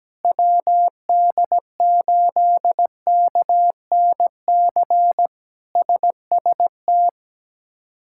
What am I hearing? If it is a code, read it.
Morse code, heard as a keyed tone, about 17 wpm, 700 Hz. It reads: WD8KNC SST
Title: K1USN SST QSO Practice - 17wpm